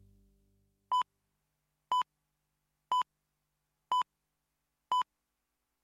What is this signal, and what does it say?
Radio time signal (time pips). It is heard from 0.9 to 5.0 s.